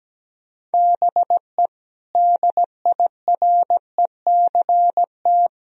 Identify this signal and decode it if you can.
Morse code, heard as a keyed tone, about 17 wpm, 700 Hz. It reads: BE DIRECT
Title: Sentences from Top 1000 Words - Review Part 2 - 17wpm